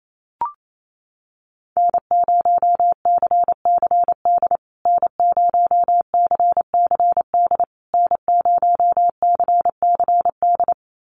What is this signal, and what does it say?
Morse code, heard as a keyed tone, about 28 wpm, 700 Hz. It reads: N0CCD N0CCD N0CCD
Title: USA Call Signs Repeated 3x 28wpm